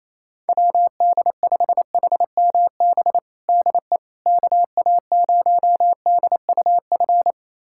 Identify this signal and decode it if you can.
Morse code, heard as a keyed tone, about 28 wpm, 700 Hz. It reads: WD5HMB DE KA0DUF
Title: USA Callsign DE Callsign 28wpm